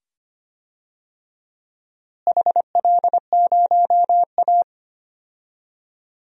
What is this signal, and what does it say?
Morse code, heard as a keyed tone, about 25 wpm, 700 Hz. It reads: HL0A
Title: DXCC-100 Call Signs - 4 Characters - Repeated 2x - 25wpm